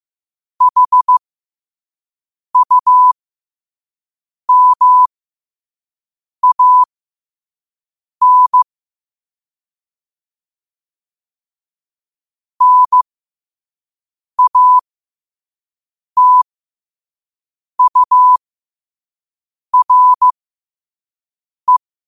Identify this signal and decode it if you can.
Morse code, heard as a keyed tone, about 15 wpm, spaced as for 5 wpm, 1000 Hz. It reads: HUMAN NATURE
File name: qotdfiles / 5wpm